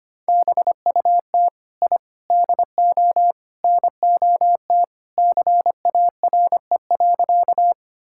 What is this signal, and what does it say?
Morse code, heard as a keyed tone, about 25 wpm, 700 Hz. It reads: BUT I DO NOT CARE.
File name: practice / 25wpm / Sentences with 4-Word Phrases - 25wpm